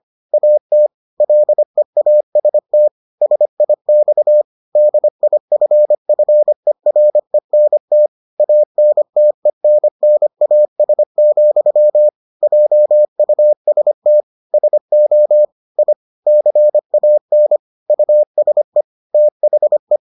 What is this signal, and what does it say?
Morse code, heard as a keyed tone, about 25 wpm, 600 Hz. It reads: AT LEAST SIX DIFFERENT ANTENNAS, JUST SO I CAN USE THE